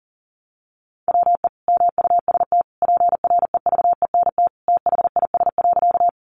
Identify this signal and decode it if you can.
Morse code, heard as a keyed tone, about 40 wpm, 700 Hz. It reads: WE MUST PREVENT THIS.
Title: Sentences from Top 1100 Words 40wpm